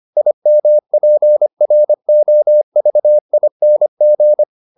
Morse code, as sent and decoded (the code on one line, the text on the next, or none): .. -- .--. .-. --- ...- .. -. --.
IMPROVING